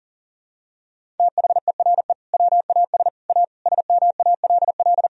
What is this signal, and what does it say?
Morse code, heard as a keyed tone, about 37 wpm, 700 Hz. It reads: THERE WAS A SMALL
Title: Sentences from Top 1000 Words - Review Part 1 - 40wpm